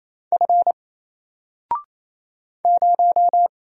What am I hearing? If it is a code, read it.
Morse code, heard as a keyed tone, about 28 wpm, 700 Hz. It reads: F 0
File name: Single-Letter-Number-28wpm